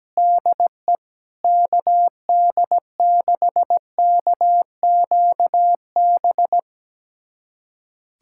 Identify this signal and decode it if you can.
Morse code, heard as a keyed tone, about 17 wpm, 700 Hz. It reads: DE KD6KQB
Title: USA Callsign DE Callsign 17wpm